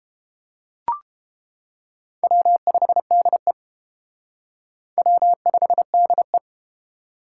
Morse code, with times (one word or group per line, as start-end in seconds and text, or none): 2.2-3.5 W5DE
5.0-6.4 W5DE